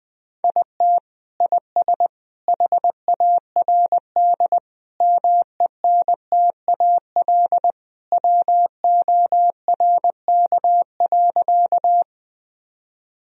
Morse code, heard as a keyed tone, about 20 wpm, 700 Hz. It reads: IT IS HARD MENTAL WORK.
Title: Sentences from Top 1000 Words - Review Part 4 - 20wpm